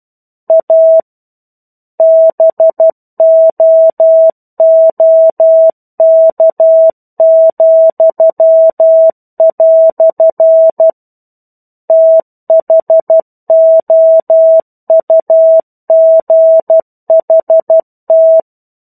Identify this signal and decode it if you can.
Morse code, heard as a keyed tone, about 12 wpm, 650 Hz. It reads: A BOOK," THOUGHT
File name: 01-Chapter-01 12 WPM